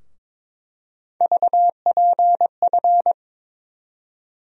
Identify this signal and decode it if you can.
Morse code, heard as a keyed tone, about 21 wpm, 700 Hz. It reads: VPF
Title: USA Call Sign Postfixes - 3 Letters 22wpm